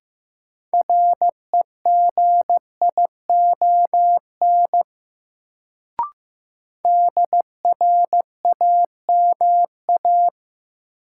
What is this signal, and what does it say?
Morse code, heard as a keyed tone, about 15 wpm, 700 Hz. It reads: REGION DRAMA
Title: Sentences from Top 1500 Words - New Word Drill Alt - 15wpm